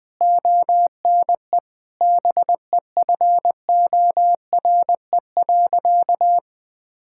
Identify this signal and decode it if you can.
Morse code, heard as a keyed tone, about 21 wpm, 700 Hz. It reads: ONE BEFORE.
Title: Sentences with 8-Word Phrases - 20wpm